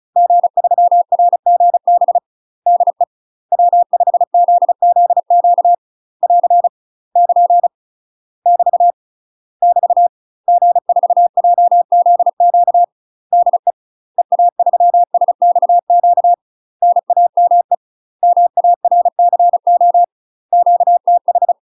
Morse code, heard as a keyed tone, about 36 wpm, 700 Hz. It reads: G3RGB DE W5ZZQ <AR> <KN> <BT> <BT> G4JZQ DE EA3SXQ NAME MARCO QTH
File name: UK MASTER QSOs TRAINING / 35WPM / UK MASTER QSOs 0018